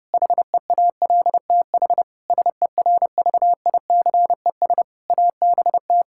Morse code, heard as a keyed tone, about 30 wpm, 700 Hz. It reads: HEALTH SERVICES ABT
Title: K1USN SST QSO Practice 30wpm